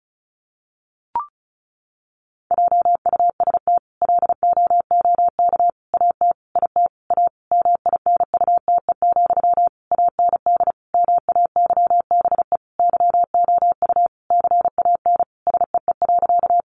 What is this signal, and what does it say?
Morse code, heard as a keyed tone, about 34 wpm, 700 Hz. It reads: JUST LOOK AT IT A MINUTE, AND MAYBE YOU CAN SEE.
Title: Sentences from Top 1000 Words 35wpm